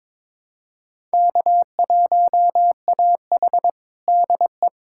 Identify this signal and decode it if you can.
Morse code, heard as a keyed tone, about 22 wpm, 700 Hz. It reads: K1AH DE
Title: USA Callsign DE Callsign - 4 Characters - 22wpm